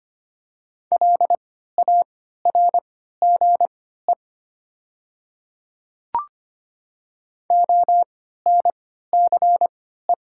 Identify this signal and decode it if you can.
Morse code, heard as a keyed tone, about 25 wpm, spaced as for 14 wpm, 690 Hz. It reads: LARGE ONCE
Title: 3x Character Spacing - Top 300 Words - 25wpm